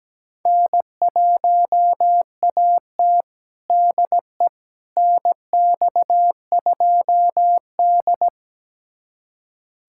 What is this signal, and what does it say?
Morse code, heard as a keyed tone, about 17 wpm, 700 Hz. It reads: N1AT DE NX2D
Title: USA Callsign DE Callsign - 4 Characters - 17wpm